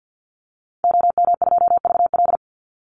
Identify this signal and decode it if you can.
Morse code, heard as a keyed tone, about 48 wpm, 700 Hz. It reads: OM2VL
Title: Call Signs Repeated 2x - 50wpm